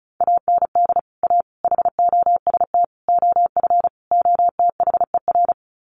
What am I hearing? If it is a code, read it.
Morse code, heard as a keyed tone, about 34 wpm, 700 Hz. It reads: AND A HOST OF OTHER
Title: Sets of 5 Words Encore2 35wpm